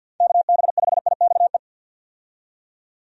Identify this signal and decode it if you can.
Morse code, heard as a keyed tone, about 52 wpm, 700 Hz. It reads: KB5IXE
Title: USA Call Signs - 6 Characters - 50wpm